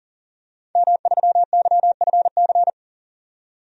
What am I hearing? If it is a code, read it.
Morse code, heard as a keyed tone, about 40 wpm, 700 Hz. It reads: M3YFC
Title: DXCC-100 Call Signs - 5 Characters - 40wpm